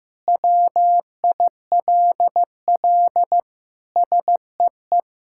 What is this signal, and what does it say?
Morse code, heard as a keyed tone, about 15 wpm, 700 Hz. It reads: WILL SEE
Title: Sentences from Top 1000 Words - Review Part 5 - 15wpm